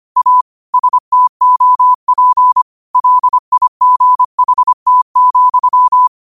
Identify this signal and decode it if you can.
Morse code, heard as a keyed tone, about 25 wpm, 1000 Hz. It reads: A STOP LIGHT,